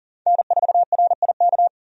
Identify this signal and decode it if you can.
Morse code, heard as a keyed tone, about 40 wpm, 700 Hz. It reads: N4RIK
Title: USA Call Signs - 5 Characters - 40wpm